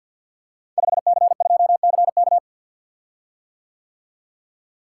Morse code, heard as a keyed tone, about 48 wpm, 700 Hz. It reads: 5C2CK
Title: Code Groups - 5 Letter:Numbers - 50wpm